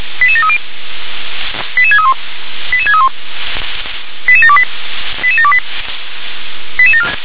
Five-tone selective call (ZVEI), heard as quick runs of five tones, tones of about 70 ms. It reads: 88730 88631 88631 88728 88728